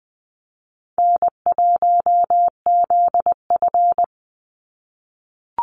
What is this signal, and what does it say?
Morse code, heard as a keyed tone, about 20 wpm, 700 Hz. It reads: N1ZF
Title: NA SOTA Chaser Callsigns 20wpm